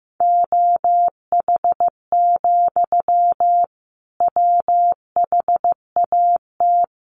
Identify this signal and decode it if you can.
Morse code, heard as a keyed tone, about 15 wpm, 700 Hz. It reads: OH, WHAT
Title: Sentences from Top 1200 Words 15wpm